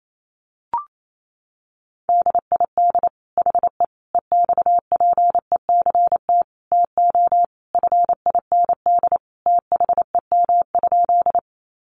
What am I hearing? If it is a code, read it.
Morse code, heard as a keyed tone, about 28 wpm, 700 Hz. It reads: DID HE EXPECT TO FIND THEM?
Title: Sentences from Top 1100 Words 28wpm